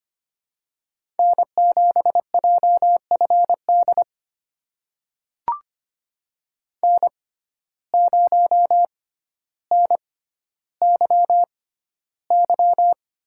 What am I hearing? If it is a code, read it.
Morse code, heard as a keyed tone, about 25 wpm, spaced as for 13 wpm, 700 Hz. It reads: N7JFD N0NYY
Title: USA Call Signs - 5 Characters - 6x Character Spacing - 25wpm